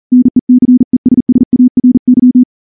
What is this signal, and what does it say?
Morse code, heard as a keyed tone, about 38 wpm, 250 Hz. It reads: NECESSARY
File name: morse / morse-training-40WPM